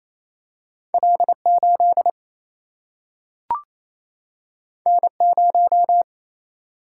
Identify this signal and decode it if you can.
Morse code, heard as a keyed tone, about 28 wpm, 700 Hz. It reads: L 8 N 0
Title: Single-Letter Single-Number 28wpm